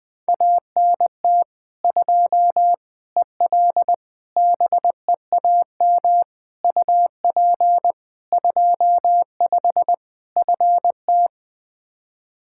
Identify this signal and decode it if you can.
Morse code, heard as a keyed tone, about 21 wpm, 700 Hz. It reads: ANT 2 EL BEAM UP 25 FT
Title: K1USN SST QSO Practice 20wpm